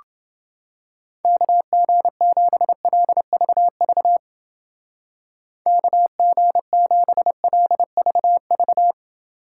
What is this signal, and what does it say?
Morse code, heard as a keyed tone, about 29 wpm, 700 Hz. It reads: KG7LVV KG7LVV